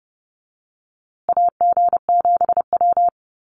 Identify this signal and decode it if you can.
Morse code, heard as a keyed tone, about 30 wpm, 700 Hz. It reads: AG7W
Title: USA Call Signs Repeated 3x 30wpm